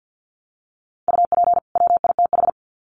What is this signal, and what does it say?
Morse code, heard as a keyed tone, about 50 wpm, 700 Hz. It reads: UP WITH
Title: Sets of 2 Words 50wpm